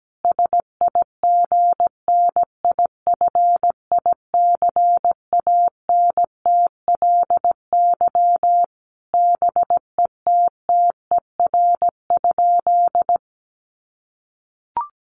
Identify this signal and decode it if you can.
Morse code, heard as a keyed tone, about 17 wpm, 700 Hz. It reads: SIGNIFICANTLY BETTER?
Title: Sentences from Top 1500 Words 17wpm